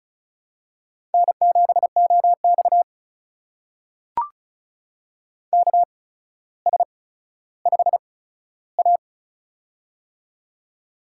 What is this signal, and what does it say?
Morse code, heard as a keyed tone, about 36 wpm, spaced as for 15 wpm, 700 Hz. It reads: N7OX KS5A
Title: USA Call Signs - 4 Characters - 8x Character Spacing - 35wpm